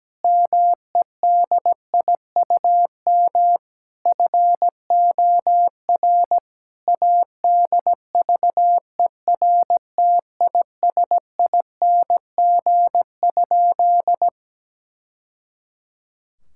Morse code, heard as a keyed tone, about 17 wpm, 700 Hz. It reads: MEDIUM FOR ADVERTISING?
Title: Sentences from Top 1000 Words - Review Part 6 - 17wpm